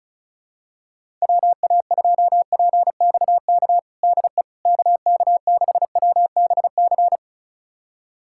Morse code, heard as a keyed tone, about 34 wpm, 700 Hz. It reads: WA2PXK DE KK6WBC
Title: USA Callsign DE Callsign - 6 Characters - 35wpm